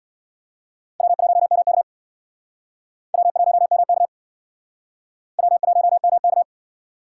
Morse code, heard as a keyed tone, about 68 wpm, 700 Hz. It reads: W1MK W1MK W1MK